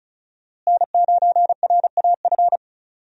Morse code, heard as a keyed tone, about 36 wpm, 700 Hz. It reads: N9RAF
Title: USA Call Signs - 5 Characters - Repeated 2x - 35wpm